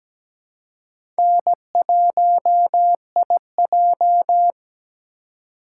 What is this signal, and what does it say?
Morse code, heard as a keyed tone, about 17 wpm, 700 Hz. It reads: N1IJ
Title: USA Call Signs - 4 Characters - 2x Character Spacing - 17wpm